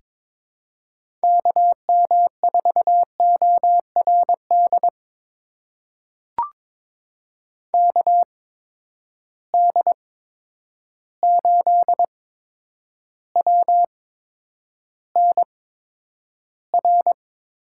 Morse code, heard as a keyed tone, about 21 wpm, spaced as for 9 wpm, 700 Hz. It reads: KM4ORD KD8WNR
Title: USA Call Signs - 6 Characters - 8x Character Spacing - 22wpm